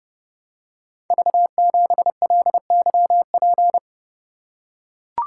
Morse code, heard as a keyed tone, about 30 wpm, 700 Hz. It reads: V7LYP
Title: Code Groups - 5 Letter:Numbers - 30wpm